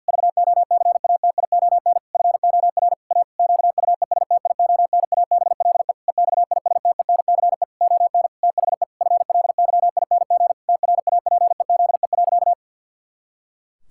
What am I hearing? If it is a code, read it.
Morse code, heard as a keyed tone, about 52 wpm, 700 Hz. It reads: VOCATION FOR A QUESTIONABLE EXISTENCE ON THE FLYING TRAPEZE.
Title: Pangram Challenge v2 - 50wpm